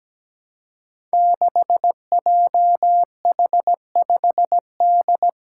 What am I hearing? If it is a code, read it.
Morse code, heard as a keyed tone, about 17 wpm, 700 Hz. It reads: 6JH5D
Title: Five Challenging Letter-Number Combinations 17wpm